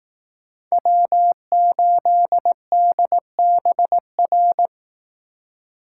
Morse code, heard as a keyed tone, about 18 wpm, 700 Hz. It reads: W8DBR
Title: Speed Racing - USA 1x3 Call Signs 15wpm